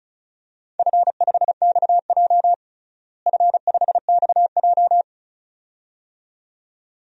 Morse code, heard as a keyed tone, about 36 wpm, 700 Hz. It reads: F5XJ F5XJ
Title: DXCC-100 Call Signs - 4 Characters - Repeated 2x - 35wpm